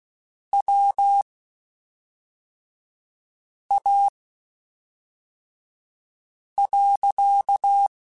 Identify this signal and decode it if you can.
Morse code, heard as a keyed tone, about 16 wpm, spaced as for 3 wpm, 800 Hz. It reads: WA.
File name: (spaced) / QSOs Spaced-mp3s / kc7haa @5 sp